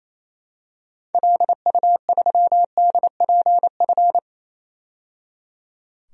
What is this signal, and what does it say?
Morse code, heard as a keyed tone, about 28 wpm, 700 Hz. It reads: LU3DPF
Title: DXCC-100 Call Signs - 6 Characters - 28wpm